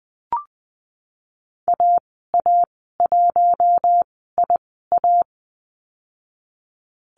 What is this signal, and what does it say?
Morse code, heard as a keyed tone, about 19 wpm, spaced as for 14 wpm, 700 Hz. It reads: AA1IA
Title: 048 Lesson 9g - Number 4 - Call Signs 20wpm